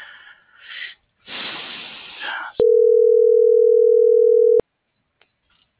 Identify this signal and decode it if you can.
Ringback tone (North America), 440 + 480 Hz.